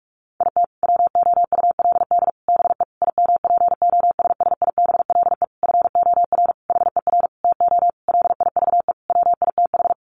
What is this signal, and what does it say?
Morse code, heard as a keyed tone, about 44 wpm, 700 Hz. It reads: IT WOULD BE IMPOSSIBLE FOR HER TO LIVE WITH
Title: Sentences from Top 1100 Words 45wpm